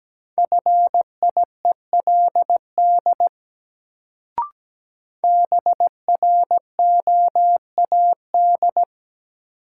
Morse code, heard as a keyed tone, about 17 wpm, 700 Hz. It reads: FIELD BROAD
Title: Sentences from Top 1000 Words - New Word Drill - 17wpm